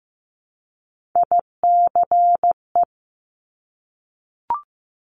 Morse code, heard as a keyed tone, about 15 wpm, 700 Hz. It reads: ICE
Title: Multi-letter Phonemes Repeated 3x 15wpm